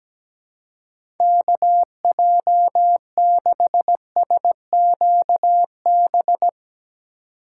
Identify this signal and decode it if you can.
Morse code, heard as a keyed tone, about 17 wpm, 700 Hz. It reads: KJ6SQB
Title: USA Call Signs - 6 Characters - 4x Character Spacing - 17wpm